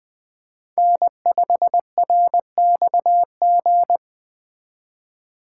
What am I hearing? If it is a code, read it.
Morse code, heard as a keyed tone, about 24 wpm, 700 Hz. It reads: N5RXG
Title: Speed Racing - USA 1x3 Call Signs 20wpm